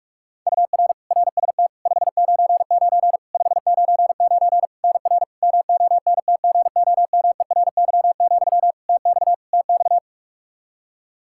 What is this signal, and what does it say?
Morse code, heard as a keyed tone, about 41 wpm, 700 Hz. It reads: UR RST 599 599 NR MONTGOMERY, TX TX
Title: Practice QSO - Exchange 1 - Beginning - 45wpm